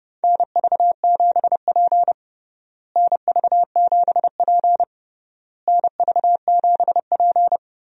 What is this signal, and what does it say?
Morse code, heard as a keyed tone, about 30 wpm, 700 Hz. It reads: NV7P NV7P NV7P
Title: USA Call Signs - 4 Characters - Repeated 3x - 30wpm